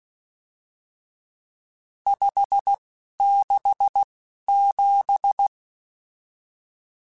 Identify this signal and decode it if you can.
Morse code, heard as a keyed tone, about 16 wpm, spaced as for 11 wpm, 800 Hz. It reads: 567
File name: RSTs / RST-mp3s / Various RST @10